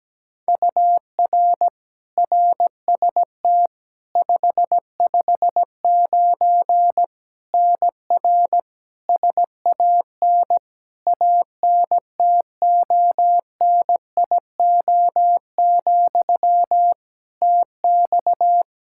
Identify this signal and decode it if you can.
Morse code, heard as a keyed tone, about 17 wpm, 700 Hz. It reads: UR RST 559 NR SAN ANTONIO, TX
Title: K1USN SST QSO Practice 17wpm